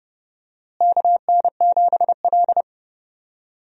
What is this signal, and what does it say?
Morse code, heard as a keyed tone, about 30 wpm, 700 Hz. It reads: KN7L
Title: USA Call Signs - 4 Characters - Repeated 3x - 30wpm